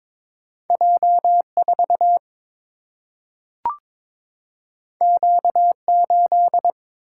Morse code, heard as a keyed tone, about 22 wpm, 700 Hz. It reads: J 4 Q 8
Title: Single-Letter Single-Number 22wpm